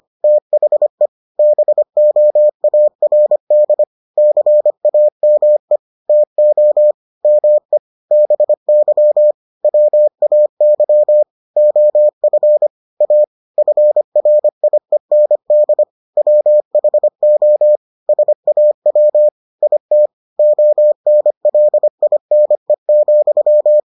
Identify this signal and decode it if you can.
Morse code, heard as a keyed tone, about 25 wpm, 600 Hz. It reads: THE BOARD CAME TO ME BY WAY OF A FRIEND WHO SAW IT ONLINE,